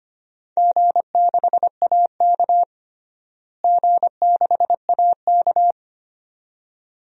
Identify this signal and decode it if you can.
Morse code, heard as a keyed tone, about 25 wpm, 700 Hz. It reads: G6AK G6AK
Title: DXCC-100 Call Signs - 4 Characters - Repeated 2x - 25wpm